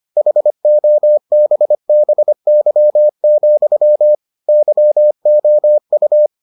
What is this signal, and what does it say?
Morse code, heard as a keyed tone, about 25 wpm, 600 Hz. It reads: HOBBY, YOU